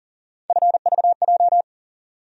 Morse code, heard as a keyed tone, about 40 wpm, 700 Hz. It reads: FVJ
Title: USA Call Sign Postfixes - 3 Letters 40wpm